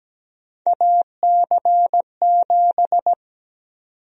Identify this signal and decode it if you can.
Morse code, heard as a keyed tone, about 17 wpm, 700 Hz. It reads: AC7
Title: USA Call Sign Prefixes - 17wpm